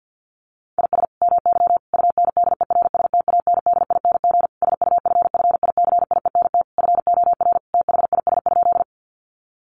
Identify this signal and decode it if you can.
Morse code, heard as a keyed tone, about 50 wpm, 700 Hz. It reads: IS MY UNDERSTANDING SUFFICIENT FOR THIS?
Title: Sentences from Top 1400 Words 50wpm